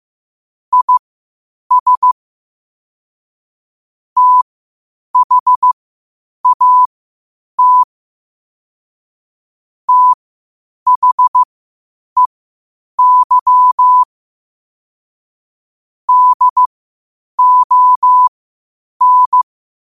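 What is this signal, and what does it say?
Morse code, heard as a keyed tone, about 15 wpm, spaced as for 8 wpm, 1000 Hz. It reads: IS THAT THEY DON
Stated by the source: Morse code quotes at 8 words per minute from QOTD at 8 WPM.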